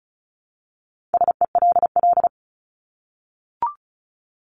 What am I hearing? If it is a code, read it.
Morse code, heard as a keyed tone, about 34 wpm, 700 Hz. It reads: SELL
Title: Top 1001-1250 Words 35wpm